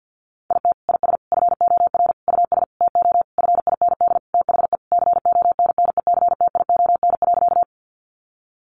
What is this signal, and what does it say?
Morse code, heard as a keyed tone, about 50 wpm, 700 Hz. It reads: IT IS FOR US TO FIND THE CONNECTION.
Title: Sentences from Top 1300 Words 50wpm